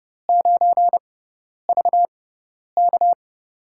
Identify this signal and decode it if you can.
Morse code, heard as a keyed tone, about 31 wpm, spaced as for 10 wpm, 700 Hz. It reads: 9VK
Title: USA Call Signs - 5 Characters - 6x Character Spacing - 30wpm